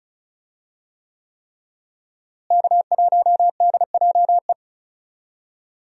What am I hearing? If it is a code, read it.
Morse code, heard as a keyed tone, about 36 wpm, 700 Hz. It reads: K1DJE
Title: USA Call Signs - 5 Characters - Repeated 3x - 35wpm